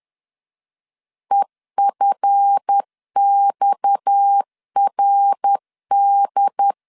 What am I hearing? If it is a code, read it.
Morse code, heard as a keyed tone, about 10 wpm, 800 Hz. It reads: EFXRD
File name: LEZIONI / 9 2 10wpm 2736